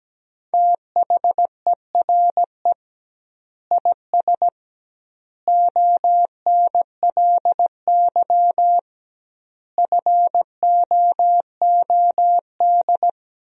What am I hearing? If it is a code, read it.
Morse code, heard as a keyed tone, about 17 wpm, 700 Hz. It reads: THERE IS ONLY FOOD
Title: 2x Word-Spacing - Sentences from Top 300 Words - 17wpm